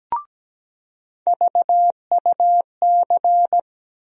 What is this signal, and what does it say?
Morse code, heard as a keyed tone, about 17 wpm, 700 Hz. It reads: VUC
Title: USA Call Sign Postfixes - 3 Letters 17wpm